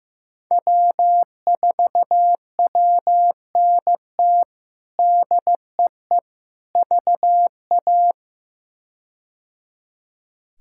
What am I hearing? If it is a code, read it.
Morse code, heard as a keyed tone, about 15 wpm, 700 Hz. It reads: W4WNT DEE VA
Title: K1USN SST QSO Practice - 15wpm